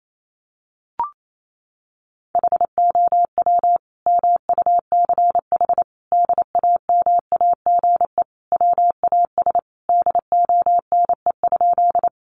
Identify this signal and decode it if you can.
Morse code, heard as a keyed tone, about 28 wpm, 700 Hz. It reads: HOW MUCH DAMAGE WAS DONE?
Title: Sentences from Top 1100 Words 28wpm